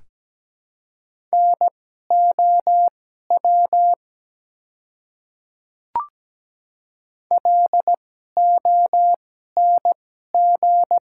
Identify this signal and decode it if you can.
Morse code, heard as a keyed tone, about 17 wpm, spaced as for 12 wpm, 700 Hz. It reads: NOW LONG
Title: 2x Character Spacing - Top 200 Words - 17wpm